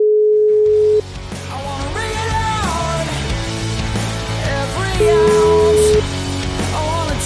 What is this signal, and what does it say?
Ringback tone (Europe, 425 Hz), with noise.